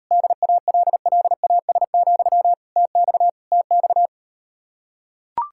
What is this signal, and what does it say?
Morse code, heard as a keyed tone, about 38 wpm, 700 Hz. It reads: DALLAS, TX TX